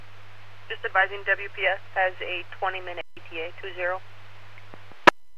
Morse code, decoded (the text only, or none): none